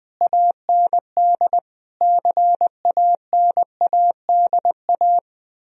Morse code, heard as a keyed tone, about 20 wpm, 700 Hz. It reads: AND CANADA